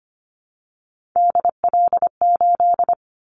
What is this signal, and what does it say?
Morse code, heard as a keyed tone, about 25 wpm, 700 Hz. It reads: DL8
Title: Callsign Prefixes for Top 50 DXCC - 25wpm